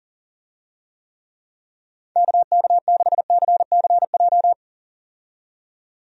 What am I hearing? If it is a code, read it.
Morse code, heard as a keyed tone, about 40 wpm, 700 Hz. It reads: KK6CCJ
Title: USA Call Signs - 6 Characters - 40wpm